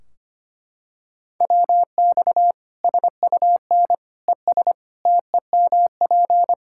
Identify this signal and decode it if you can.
Morse code, heard as a keyed tone, about 25 wpm, 700 Hz. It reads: WX SUN ES TEMP
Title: K1USN SST QSO Practice 25wpm